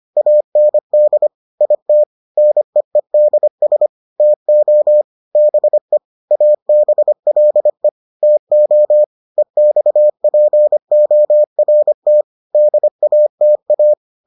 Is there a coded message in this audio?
Yes: Morse code from 0.2 to 13.9 s, heard as a keyed tone, about 25 wpm, 600 Hz. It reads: AND IT NEEDS TO BE ABLE TO EXPORT DATA